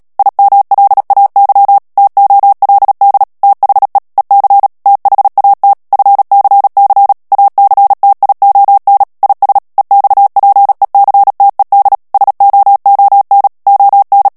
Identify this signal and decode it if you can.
Morse code, heard as a keyed tone, about 37 wpm, 800 Hz. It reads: IMLAY TOLD THE EC THAT FCC ACTION IS EXPECTED SOON ON
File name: morse mp3 / ANGLAIS texte clair 3 vitesse 35 mots minute